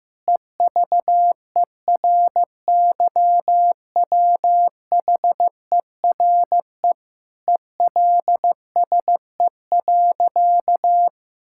Morse code, heard as a keyed tone, about 16 wpm, 700 Hz. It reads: EVERYWHERE ELSE.
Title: Sentences from Top 1600 Words 15wpm